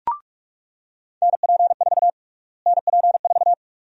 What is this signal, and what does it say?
Morse code, heard as a keyed tone, about 46 wpm, 700 Hz. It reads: NP4 NP4
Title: USA Call Sign Prefixes - Repeated 2x - 45wpm